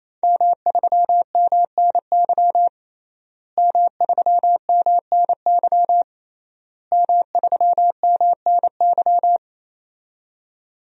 Morse code, heard as a keyed tone, about 28 wpm, 700 Hz. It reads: M3MNY M3MNY M3MNY
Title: DXCC-100 Call Signs - 5 Characters - Repeated 3x - 28wpm